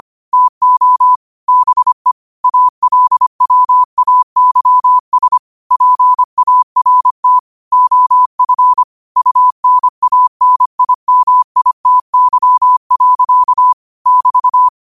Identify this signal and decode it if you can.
Morse code, heard as a keyed tone, about 25 wpm, 1000 Hz. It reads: TO BE ALWAYS PART OF UNANIMITY. <BT>